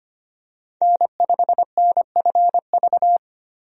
Morse code, heard as a keyed tone, about 25 wpm, 700 Hz. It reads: N5NFV
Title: USA Call Signs - 5 Characters - 25wpm